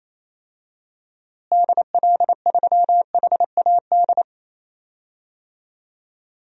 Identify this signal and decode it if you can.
Morse code, heard as a keyed tone, about 28 wpm, 700 Hz. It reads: DL3HAD
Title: DXCC-100 Call Signs - 6 Characters - Repeated 2x - 28wpm